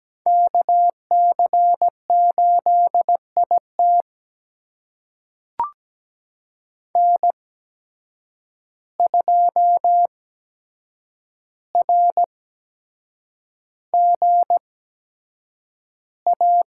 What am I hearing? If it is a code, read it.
Morse code, heard as a keyed tone, about 17 wpm, spaced as for 7 wpm, 700 Hz. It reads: KC8IT N2RGA
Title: USA Call Signs - 5 Characters - 8x Character Spacing - 17wpm